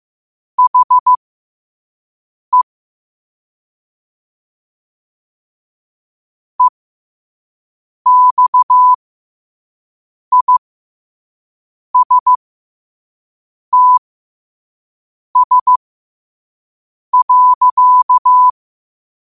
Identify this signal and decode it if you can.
Morse code, heard as a keyed tone, about 15 wpm, spaced as for 5 wpm, 1000 Hz. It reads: HE EXISTS.